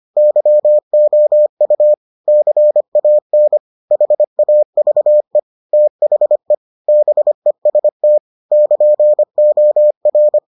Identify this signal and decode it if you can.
Morse code, heard as a keyed tone, about 25 wpm, 600 Hz. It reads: YOU CAN HAVE THE BEST <KN>OR